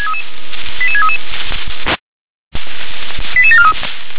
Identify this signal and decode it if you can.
Five-tone selective call (ZVEI), heard as quick runs of five tones, tones of about 80 ms. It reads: 88633 88743